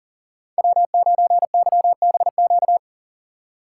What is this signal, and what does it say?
Morse code, heard as a keyed tone, about 40 wpm, 700 Hz. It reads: W9YBQ
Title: USA Call Signs - 5 Characters - 4x Character Spacing - 40wpm